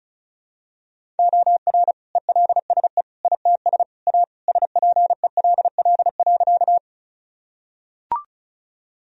Morse code, heard as a keyed tone, about 34 wpm, 700 Hz. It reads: OR ELSE ITS A SPELL.
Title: Sentences from Top 1000 Words - Review Part 1 - 35wpm